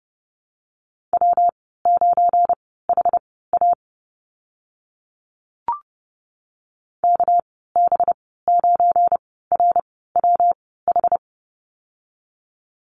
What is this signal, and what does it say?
Morse code, heard as a keyed tone, about 30 wpm, spaced as for 17 wpm, 700 Hz. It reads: W9HA KB9RWH
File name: Lesson 37g - Letter x - Call Signs 30wpm